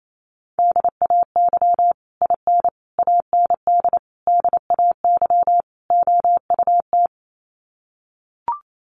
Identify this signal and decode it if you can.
Morse code, heard as a keyed tone, about 28 wpm, 700 Hz. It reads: DAY IN AND DAY OUT
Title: Sets of 5 Words Encore2 28wpm